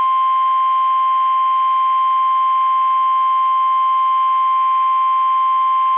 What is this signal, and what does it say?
Emergency Alert System (SAME) tones (the 1050 Hz attention tone). It is heard from 1.1 to 5.6 s.
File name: eas / KUSO / KUSO